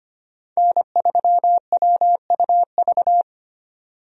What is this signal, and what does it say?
Morse code, heard as a keyed tone, about 25 wpm, 690 Hz. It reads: N3WUV
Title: USA Call Signs - 5 Characters - 25wpm